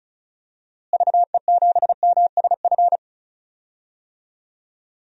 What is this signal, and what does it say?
Morse code, heard as a keyed tone, about 34 wpm, 700 Hz. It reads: VE7MSF
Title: DXCC-100 Call Signs - 6 Characters - Repeated 2x - 35wpm